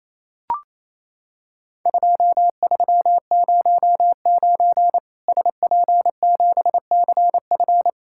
Morse code, heard as a keyed tone, about 28 wpm, 700 Hz. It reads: 2309 SP7CF
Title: Worked All Europe DX Contest - QTC Exchange - 28wpm